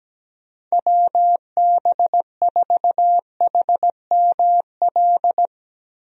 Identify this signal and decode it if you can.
Morse code, heard as a keyed tone, about 17 wpm, 700 Hz. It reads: WB4HML
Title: USA Call Signs - 6 Characters - Repeated 2x - 17wpm